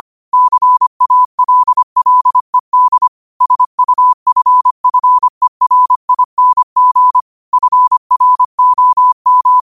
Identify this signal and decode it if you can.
Morse code, heard as a keyed tone, about 25 wpm, 1000 Hz. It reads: CALLED SUFFERING FROM